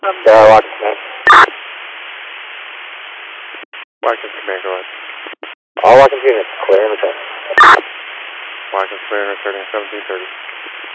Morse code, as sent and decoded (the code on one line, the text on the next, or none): none